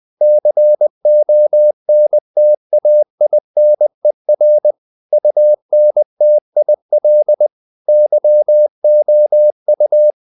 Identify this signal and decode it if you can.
Morse code, heard as a keyed tone, about 20 wpm, 600 Hz. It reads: CONTAINER UNTIL YOU
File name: the old man and the sea / the old man and the sea part 1 20wpm